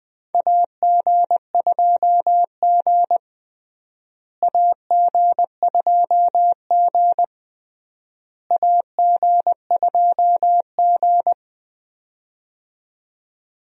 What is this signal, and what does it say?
Morse code, heard as a keyed tone, about 20 wpm, 700 Hz. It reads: AG2G AG2G AG2G
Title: USA Call Signs - 4 Characters - Repeated 3x - 20wpm